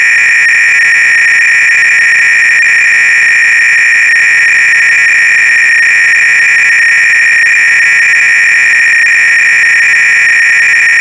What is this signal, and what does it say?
300-baud modem data.
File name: ACR-Audio-Cassette-Programs / CharlesSwiger-Donations / GOBLINS cload-g